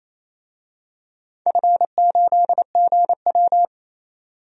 Morse code, heard as a keyed tone, about 28 wpm, 700 Hz. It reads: F8GW
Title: DXCC-100 Call Signs - 4 Characters - 28wpm